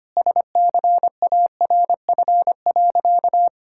Morse code, heard as a keyed tone, about 25 wpm, 700 Hz. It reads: SCARF.
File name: Easy Sentences with Hints - 25wpm